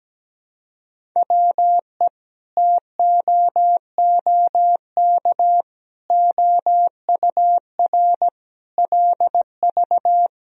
Morse code, heard as a keyed tone, about 17 wpm, 700 Hz. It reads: WE TOOK OUR LV
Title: Sentences from Top 400 Words and Abbreviations 17wpm